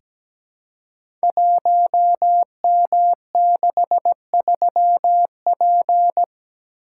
Morse code, heard as a keyed tone, about 17 wpm, 700 Hz. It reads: 1M63P
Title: Code Groups - 5 Letter:Numbers - 17wpm